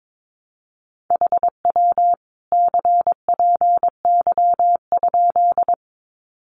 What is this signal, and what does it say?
Morse code, heard as a keyed tone, about 22 wpm, 700 Hz. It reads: HW CPY?
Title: Common QSO Phrases 22wpm